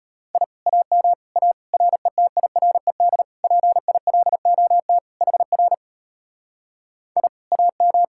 Morse code, heard as a keyed tone, about 33 wpm, 700 Hz. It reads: I AM A RETIRED PILOT HR I AM